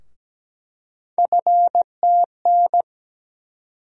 Morse code, heard as a keyed tone, about 17 wpm, 700 Hz. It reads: FTN
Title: USA Call Sign Postfixes - 3 Letters 17wpm